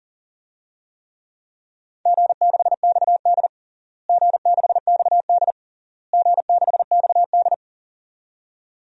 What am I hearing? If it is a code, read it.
Morse code, heard as a keyed tone, about 37 wpm, 700 Hz. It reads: G6XD G6XD G6XD
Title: DXCC-100 Call Signs - 4 Characters - Repeated 3x - 40wpm